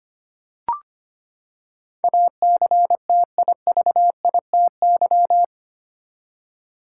Morse code, heard as a keyed tone, about 25 wpm, 700 Hz. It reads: ACTIVITY
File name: Eight Letter Words - 25wpm